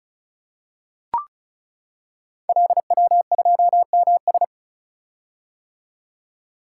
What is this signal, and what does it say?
Morse code, heard as a keyed tone, about 36 wpm, 700 Hz. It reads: LW2MS
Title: Code Groups - 5 Letter:Numbers - 35wpm